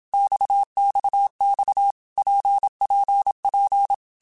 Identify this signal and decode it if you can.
Morse code, heard as a keyed tone, about 27 wpm, 800 Hz. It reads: XXX PPP
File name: Basic Letter Groups By7Groups / RKPX / RKPX@25